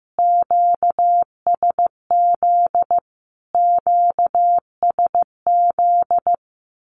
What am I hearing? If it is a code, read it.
Morse code, heard as a keyed tone, about 15 wpm, 700 Hz. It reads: QSZ QSZ
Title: Q-Codes Repeated 3x 15wpm